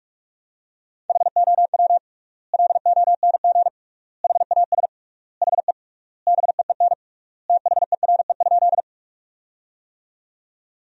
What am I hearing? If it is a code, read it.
Morse code, heard as a keyed tone, about 46 wpm, 700 Hz. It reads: HOW LONG HAS HE BEEN THERE?
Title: 3x Word-Spacing - Sentences from Top 100 Words - 45wpm